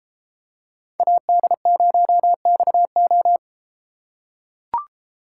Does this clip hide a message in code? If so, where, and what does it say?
Yes: Morse code from 1.0 to 3.4 s, heard as a keyed tone, about 33 wpm, 700 Hz. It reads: AD0XO